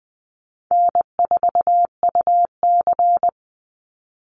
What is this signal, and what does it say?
Morse code, heard as a keyed tone, about 20 wpm, 700 Hz. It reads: N4UC
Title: Call Signs - 20wpm